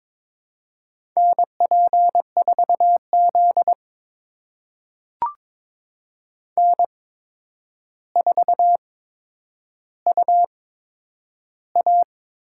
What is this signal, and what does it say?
Morse code, heard as a keyed tone, about 22 wpm, spaced as for 9 wpm, 700 Hz. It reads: NP4Z N4UA
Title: USA Call Signs - 4 Characters - 8x Character Spacing - 22wpm